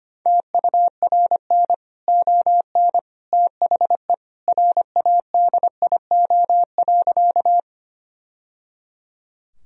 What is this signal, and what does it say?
Morse code, heard as a keyed tone, about 25 wpm, 700 Hz. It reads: TURN ON THE RADIO.
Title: Sentences from Top 1000 Words - Review Part 3 - 25wpm